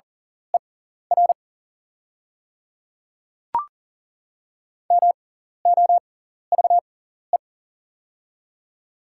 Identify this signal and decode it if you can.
Morse code, heard as a keyed tone, about 40 wpm, spaced as for 14 wpm, 700 Hz. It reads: ER MOVE
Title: 6x Character Spacing - Top 500 Words - 40wpm